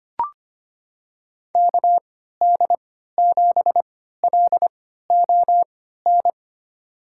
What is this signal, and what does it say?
Morse code, heard as a keyed tone, about 25 wpm, spaced as for 14 wpm, 700 Hz. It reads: KD7LON
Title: USA Call Signs - 6 Characters - 3x Character Spacing - 25wpm